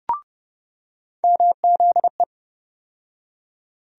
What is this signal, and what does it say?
Morse code, heard as a keyed tone, about 30 wpm, 700 Hz. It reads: MZE
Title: USA Call Sign Postfixes - 3 Letters 30wpm